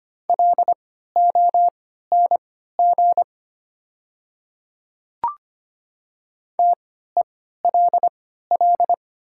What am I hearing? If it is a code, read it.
Morse code, heard as a keyed tone, about 25 wpm, spaced as for 14 wpm, 690 Hz. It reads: LONG TELL
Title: 3x Character Spacing - Top 200 Words - 25wpm